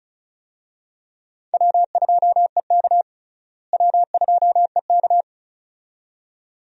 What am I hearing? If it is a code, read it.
Morse code, heard as a keyed tone, about 34 wpm, 700 Hz. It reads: W2EK W2EK
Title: USA Call Signs - 4 Characters - Repeated 2x - 35wpm